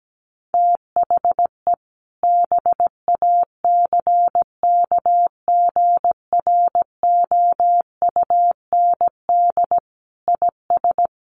Morse code, heard as a keyed tone, about 17 wpm, 700 Hz. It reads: THE BACKGROUND IS
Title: Sentences from Top 1300 Words 17wpm